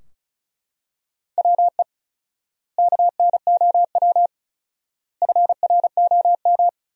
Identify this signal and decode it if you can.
Morse code, heard as a keyed tone, about 36 wpm, 700 Hz. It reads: WE KNOW FROM
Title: 4x Word-Spacing - Sentences from Top 500 Words - 35wpm